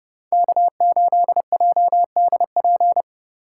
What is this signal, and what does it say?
Morse code, heard as a keyed tone, about 27 wpm, 700 Hz. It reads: K8JDP